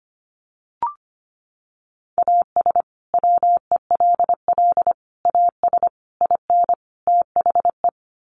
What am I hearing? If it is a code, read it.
Morse code, heard as a keyed tone, about 25 wpm, 700 Hz. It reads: AS WELL AS IN THE
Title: Sets of 5 Words Encore 25wpm